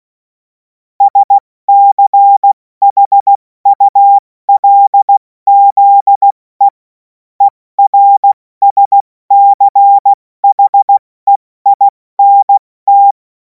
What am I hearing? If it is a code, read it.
Morse code, heard as a keyed tone, about 16 wpm, 800 Hz. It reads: SCHULZE ERSCHEINT
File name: Morsekurs / Witze / Witz 26. Mai 2026 70 F80